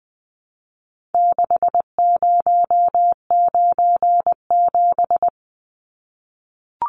Morse code, heard as a keyed tone, about 20 wpm, 700 Hz. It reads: 6097
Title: Numbers - 4-Digit 20wpm